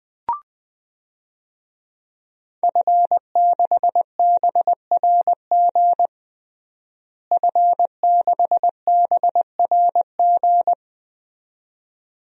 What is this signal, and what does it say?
Morse code, heard as a keyed tone, about 20 wpm, 700 Hz. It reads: F6BRG F6BRG
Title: DXCC-100 Call Signs - 5 Characters - Repeated 2x - 20wpm